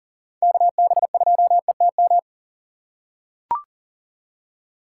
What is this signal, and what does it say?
Morse code, heard as a keyed tone, about 38 wpm, 700 Hz. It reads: KB2ETM
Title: USA Call Signs - 6 Characters - 6x Character Spacing - 40wpm